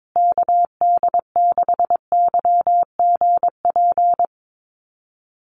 Morse code, heard as a keyed tone, about 22 wpm, 700 Hz. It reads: KD6YGP
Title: USA Call Signs Repeated 3x 22wpm